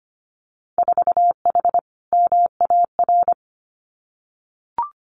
Morse code, heard as a keyed tone, about 25 wpm, 700 Hz. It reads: 4H MAR
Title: Winter Field Day Exchanges Encore 25wpm